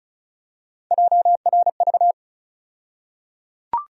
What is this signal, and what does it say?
Morse code, heard as a keyed tone, about 36 wpm, 700 Hz. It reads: JRV
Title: USA Call Sign Postfixes - 3 Letters 35wpm